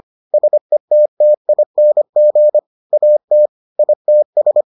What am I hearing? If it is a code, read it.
Morse code, heard as a keyed tone, about 25 wpm, 600 Hz. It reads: SETTING AT ITS